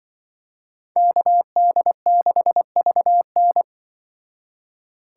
Morse code, heard as a keyed tone, about 24 wpm, 700 Hz. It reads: KD6VN
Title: Speed Racing - USA Call Signs 20wpm